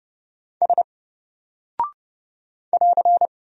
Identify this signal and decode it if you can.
Morse code, heard as a keyed tone, about 28 wpm, 700 Hz. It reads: S <AR>
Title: Single Character (Letter:Number:Punctuation:Prosign) 30wpm